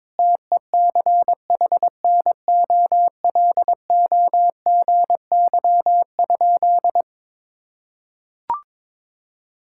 Morse code, heard as a keyed tone, about 22 wpm, 700 Hz. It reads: TECHNOLOGY?
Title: Sentences from Top 1000 Words - Review Part 5 - 22wpm